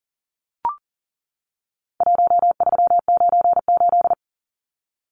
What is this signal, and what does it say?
Morse code, heard as a keyed tone, about 40 wpm, 700 Hz. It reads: 1398
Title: Numbers - Contest Style - 4-Digit 40wpm